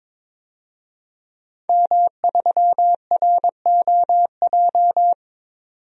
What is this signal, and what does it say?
Morse code, heard as a keyed tone, about 22 wpm, 700 Hz. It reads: M3ROJ
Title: DXCC-100 Call Signs - 5 Characters - 22wpm